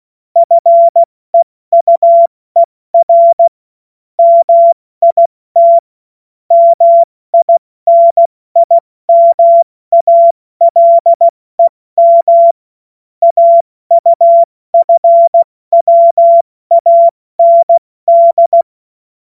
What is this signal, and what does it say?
Morse code, heard as a keyed tone, about 16 wpm, 680 Hz. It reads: FEUER MIT MINIMALEM AUFWAND
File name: Morsekurs / Witze / Witz 14. Juni 2022 70 F80